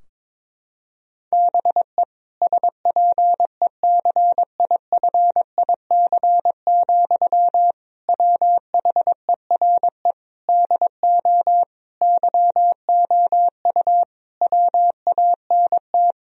Morse code, heard as a keyed tone, about 22 wpm, 700 Hz. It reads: BE SPECIFIC, WHERE DO YOU WANT
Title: Sentences from Top 1000 Words - Review Part 3 - 22wpm